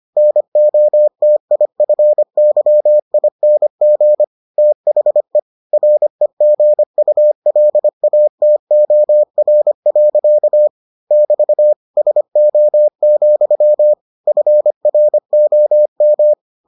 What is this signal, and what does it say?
Morse code, heard as a keyed tone, about 25 wpm, 600 Hz. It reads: NOTIFYING THE REGULATOR. <BT> SO, FROM